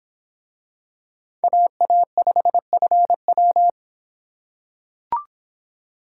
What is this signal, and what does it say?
Morse code, heard as a keyed tone, about 26 wpm, 700 Hz. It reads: AA5FW